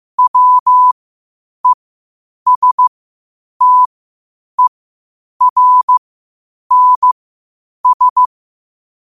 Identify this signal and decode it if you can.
Morse code, heard as a keyed tone, about 15 wpm, spaced as for 8 wpm, 1000 Hz. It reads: WESTERNS
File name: qotdfiles / 8wpm